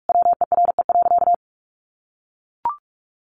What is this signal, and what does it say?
Morse code, heard as a keyed tone, about 44 wpm, 700 Hz. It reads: WERE.
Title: Sentences from Top 1000 Words 45wpm